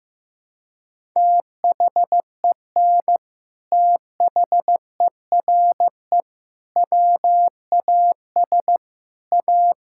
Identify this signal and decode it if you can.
Morse code, heard as a keyed tone, about 15 wpm, 700 Hz. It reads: THEN THERE WAS A
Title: Sentences from Top 1600 Words 15wpm